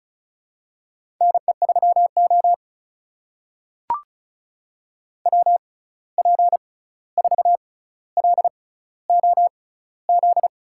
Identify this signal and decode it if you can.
Morse code, heard as a keyed tone, about 36 wpm, spaced as for 16 wpm, 700 Hz. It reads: NE3O WP4LOZ
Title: USA Call Signs - 6x Character Spacing - 35wpm